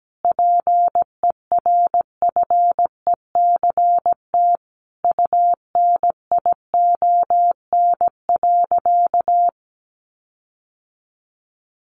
Morse code, heard as a keyed tone, about 17 wpm, 700 Hz. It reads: PERFECT UNION.
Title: Sentences from Top 1400 Words 17wpm